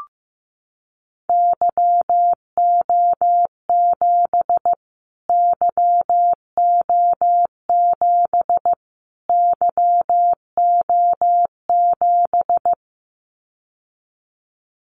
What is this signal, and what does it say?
Morse code, heard as a keyed tone, about 15 wpm, 700 Hz. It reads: YO7 YO7 YO7
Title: Callsign Prefixes for Top 25 DXCC - Repeated 3x - 15wpm